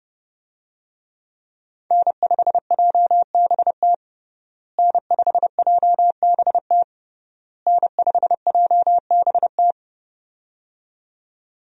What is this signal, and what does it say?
Morse code, heard as a keyed tone, about 30 wpm, 700 Hz. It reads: N5JBT N5JBT N5JBT
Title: USA Call Signs - 5 Characters - Repeated 3x - 30wpm